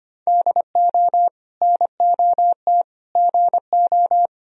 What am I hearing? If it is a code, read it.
Morse code, heard as a keyed tone, about 25 wpm, 700 Hz. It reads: DO NOT GO
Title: Sentences from Top 1000 Words - Review Part 3 - 25wpm